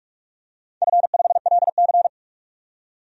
Morse code, heard as a keyed tone, about 43 wpm, 700 Hz. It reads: F5LC
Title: DXCC-100 Call Signs - 4 Characters - 45wpm